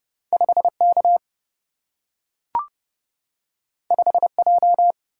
Morse code, heard as a keyed tone, about 30 wpm, 700 Hz. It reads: 5K 5J
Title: Single-Letter Single-Number 30wpm